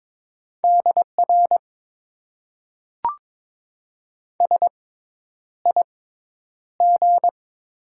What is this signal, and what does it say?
Morse code, heard as a keyed tone, about 22 wpm, spaced as for 10 wpm, 700 Hz. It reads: DR SIG
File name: Top 90 QSO Elements - 6x Character Spacing - 22wpm